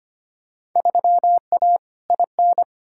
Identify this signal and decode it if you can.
Morse code, heard as a keyed tone, about 25 wpm, 700 Hz. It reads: 3A IN
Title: Field Day Exchanges Encore - 25wpm